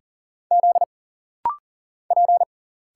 Morse code, heard as a keyed tone, about 40 wpm, 700 Hz. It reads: Z P
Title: Single Letter-Number - Rapid-Fire 40wpm